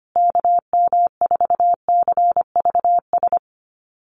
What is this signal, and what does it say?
Morse code, heard as a keyed tone, about 25 wpm, 700 Hz. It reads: KM4CVS
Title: USA Call Signs Repeated 3x 25wpm